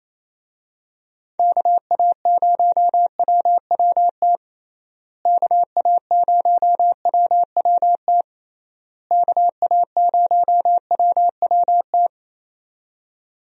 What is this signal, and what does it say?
Morse code, heard as a keyed tone, about 27 wpm, 700 Hz. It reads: KA0WWT KA0WWT KA0WWT
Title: USA Call Signs - 6 Characters - Repeated 3x - 28wpm